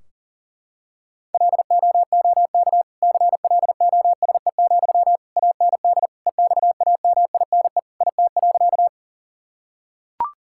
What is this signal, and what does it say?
Morse code, heard as a keyed tone, about 40 wpm, 700 Hz. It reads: LOOK CLOSE, AND EXAMINE IT.
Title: Sentences from Top 1600 Words 40wpm